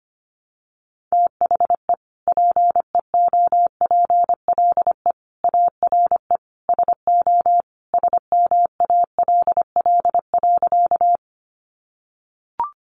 Morse code, heard as a keyed tone, about 25 wpm, 700 Hz. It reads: THE PEOPLE ARE SO SMALL.
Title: Additional Easy Phrases 25wpm